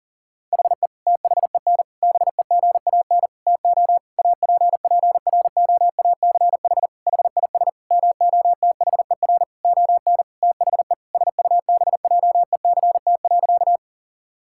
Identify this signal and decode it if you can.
Morse code, heard as a keyed tone, about 40 wpm, 700 Hz. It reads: HE THEN BEGAN TO APPROACH HIS MOTHER ON THE SUBJECT.
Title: Sentences from Top 1000 Words - Review Part 6 - 40wpm